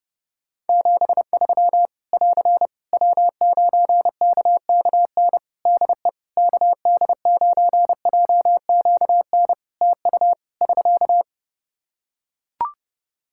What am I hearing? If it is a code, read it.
Morse code, heard as a keyed tone, about 30 wpm, 700 Hz. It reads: 73 <AR> W9KKN DE KD9JQN TU <SK>
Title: Practice QSO - Exchange 4 - Ending - 30wpm